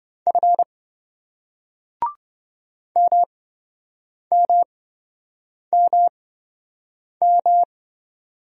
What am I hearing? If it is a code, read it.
Morse code, heard as a keyed tone, about 27 wpm, spaced as for 7 wpm, 700 Hz. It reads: F MMMM